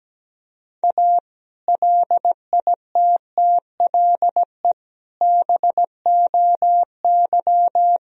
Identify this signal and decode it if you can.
Morse code, heard as a keyed tone, about 17 wpm, 700 Hz. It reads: A LITTLE BOY
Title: Sentences from Top 1000 Words - Review Part 1 - 17wpm